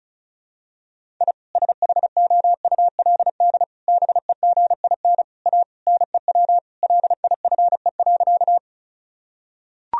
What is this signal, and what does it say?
Morse code, heard as a keyed tone, about 34 wpm, 700 Hz. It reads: I SHOULD BEGIN A NEW LIFE.
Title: Sentences from Top 1000 Words - Review Part 1 - 35wpm